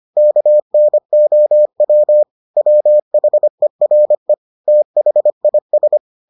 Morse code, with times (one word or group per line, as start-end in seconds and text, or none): 0.2-2.2 KNOW
2.6-4.3 WHERE
4.7-6.0 THIS